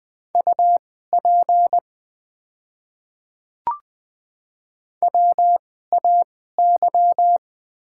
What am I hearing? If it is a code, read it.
Morse code, heard as a keyed tone, about 20 wpm, spaced as for 15 wpm, 700 Hz. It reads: UP WAY
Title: 2x Character Spacing - Top 100 Words - 20wpm